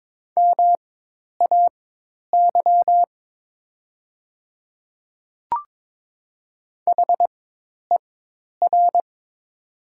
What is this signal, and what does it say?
Morse code, heard as a keyed tone, about 22 wpm, spaced as for 10 wpm, 700 Hz. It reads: MAY HER
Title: 4x Character Spacing - Top 200 Words - 22wpm